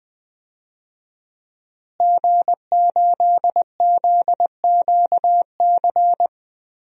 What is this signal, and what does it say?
Morse code, heard as a keyed tone, about 21 wpm, 700 Hz. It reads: G8ZQC
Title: DXCC-100 Call Signs - 5 Characters - Repeated 3x - 20wpm